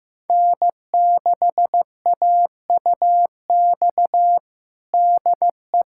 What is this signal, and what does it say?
Morse code, heard as a keyed tone, about 15 wpm, 700 Hz. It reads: N6AUX DE
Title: USA Callsign DE Callsign - 5 Characters - 15wpm